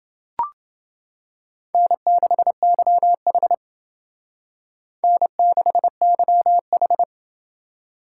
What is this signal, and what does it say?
Morse code, heard as a keyed tone, about 28 wpm, 700 Hz. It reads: N6YH N6YH